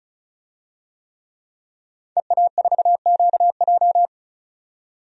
Morse code, heard as a keyed tone, about 36 wpm, 700 Hz. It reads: EA4QJ
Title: DXCC-100 Call Signs - 5 Characters - 35wpm